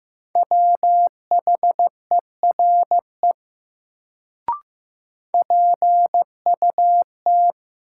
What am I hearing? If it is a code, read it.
Morse code, heard as a keyed tone, about 15 wpm, 700 Hz. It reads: WHERE PUT
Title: Sentences from Top 200 Words - New Word Drill - 15wpm